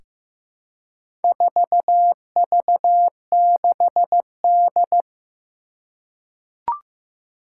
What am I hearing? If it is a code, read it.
Morse code, heard as a keyed tone, about 15 wpm, 700 Hz. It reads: 4V6D
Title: Four Challenging Letter-Number Combinations 15wpm